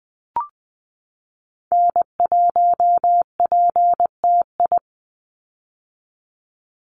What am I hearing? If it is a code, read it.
Morse code, heard as a keyed tone, about 20 wpm, 700 Hz. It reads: N1PTI
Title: USA Call Signs 20wpm